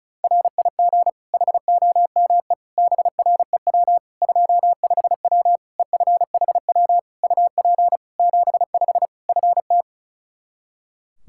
Morse code, heard as a keyed tone, about 34 wpm, 700 Hz. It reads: RIG HOME BREW 25W EFHW UP 75 FT
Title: K1USN SST QSO Practice 35wpm